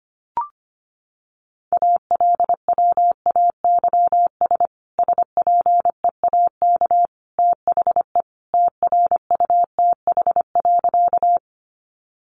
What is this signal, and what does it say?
Morse code, heard as a keyed tone, about 25 wpm, 700 Hz. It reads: ALWAYS SPEAK THE TRUTH.
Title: Sentences from Top 1100 Words 25wpm